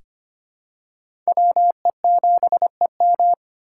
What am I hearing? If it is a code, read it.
Morse code, heard as a keyed tone, about 25 wpm, 700 Hz. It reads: WE7EM
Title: USA Call Signs - 5 Characters - 8x Character Spacing - 25wpm